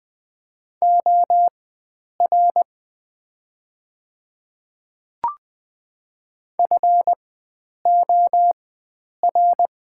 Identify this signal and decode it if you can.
Morse code, heard as a keyed tone, about 20 wpm, spaced as for 9 wpm, 700 Hz. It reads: OR FOR
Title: 4x Character Spacing - Top 500 Words - 20wpm